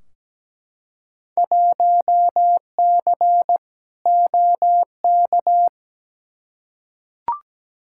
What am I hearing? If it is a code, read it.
Morse code, heard as a keyed tone, about 17 wpm, 700 Hz. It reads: 1C OK
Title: Field Day Exchanges Encore - 17wpm